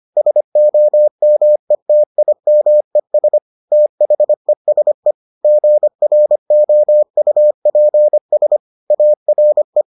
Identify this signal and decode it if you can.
Morse code, heard as a keyed tone, about 25 wpm, 600 Hz. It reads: SOMETIMES THESE GROUPS ARE